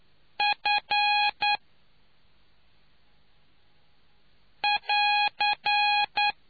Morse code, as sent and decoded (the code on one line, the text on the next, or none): ..-. .-.-.
F<AR>